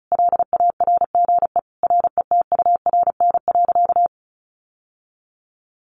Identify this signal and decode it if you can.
Morse code, heard as a keyed tone, about 34 wpm, 700 Hz. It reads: LARGE RETURN.
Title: Sentences from Top 1100 Words 35wpm